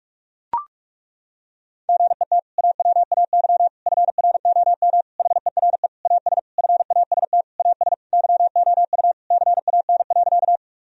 Morse code, heard as a keyed tone, about 41 wpm, 700 Hz. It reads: GET AWAY FROM HERE AS FAST AS YOU CAN.
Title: Sentences from Top 1000 Words - Review Part 2 - 45wpm